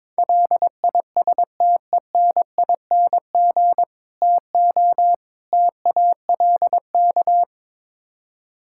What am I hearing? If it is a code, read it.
Morse code, heard as a keyed tone, about 22 wpm, 700 Hz. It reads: LISTENING TO TALK
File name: practice / 22wpm / Build A Phrase - 5-Word Phrases - 22wpm